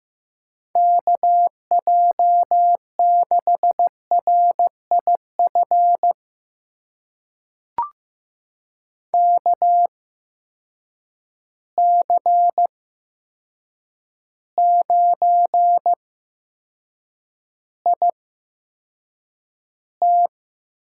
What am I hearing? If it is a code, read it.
Morse code, heard as a keyed tone, about 15 wpm, spaced as for 7 wpm, 700 Hz. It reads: KJ6RIF KC9IT
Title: USA Call Signs - 6 Characters - 8x Character Spacing - 15wpm